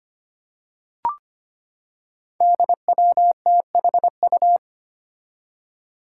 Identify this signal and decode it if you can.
Morse code, heard as a keyed tone, about 25 wpm, 700 Hz. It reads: DWTHU
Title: Five Challenging Letter-Number Combinations 25wpm